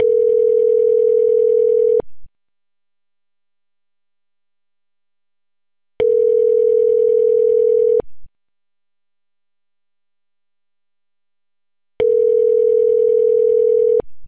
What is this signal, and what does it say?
Ringback tone (North America), 440 + 480 Hz.